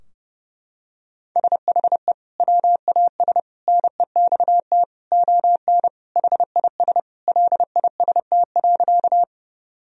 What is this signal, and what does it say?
Morse code, heard as a keyed tone, about 30 wpm, 700 Hz. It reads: SHE WAS NEXT ON HIS LIST.
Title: Sentences from Top 1000 Words - Review Part 2 - 30wpm